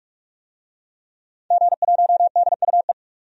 Morse code, heard as a keyed tone, about 46 wpm, 700 Hz. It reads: G1DUE
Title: DXCC-100 Call Signs - 5 Characters - 45wpm